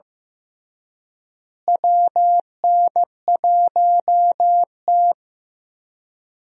Morse code, heard as a keyed tone, about 15 wpm, 700 Hz. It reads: WN1T
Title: USA Call Signs - 4 Characters - Repeated 3x - 15wpm